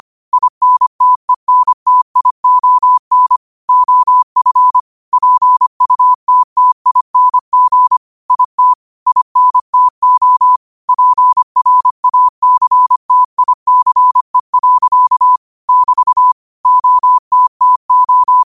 Morse code, heard as a keyed tone, about 25 wpm, 1000 Hz. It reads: INTENTION OF PUTTING IT INTO PRACTICE. <BT> OTTO
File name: qotdfiles / 25wpm